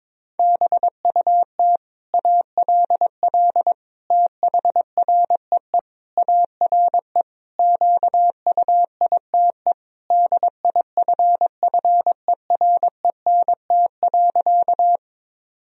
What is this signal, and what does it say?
Morse code, heard as a keyed tone, about 22 wpm, 700 Hz. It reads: BUT ALL THREE ARE QUITE DIFFERENT.
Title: Sentences from Top 1000 Words - Review Part 3 - 22wpm